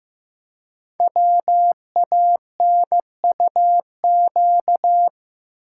Morse code, heard as a keyed tone, about 15 wpm, 700 Hz. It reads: WANUQ
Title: Code Groups - 5 Letter:Numbers - 15wpm